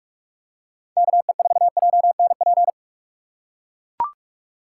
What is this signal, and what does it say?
Morse code, heard as a keyed tone, about 43 wpm, 700 Hz. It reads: KE4JNP